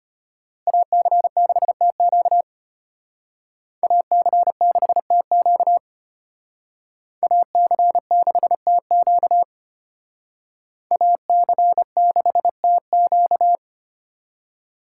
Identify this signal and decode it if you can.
Morse code, heard as a keyed tone, about 32 wpm, 700 Hz. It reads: AC6TQ AC6TQ AC6TQ AC6TQ